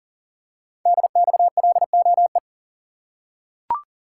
Morse code, heard as a keyed tone, about 40 wpm, 700 Hz. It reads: DXLOE
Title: Code Groups - 5 Letters - 40wpm